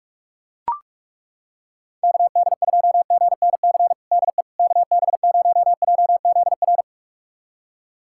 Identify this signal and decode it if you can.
Morse code, heard as a keyed tone, about 42 wpm, 700 Hz. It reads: KD2GNC DE KB0JZR
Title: USA Callsign DE Callsign - 6 Characters - 45wpm